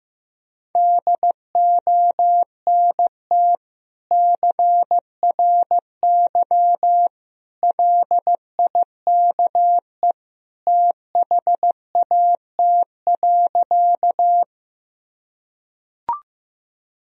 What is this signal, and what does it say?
Morse code, heard as a keyed tone, about 15 wpm, 700 Hz. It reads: DONT CRY LIKE THAT.
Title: Sentences from Top 1000 Words - Review Part 2 - 15wpm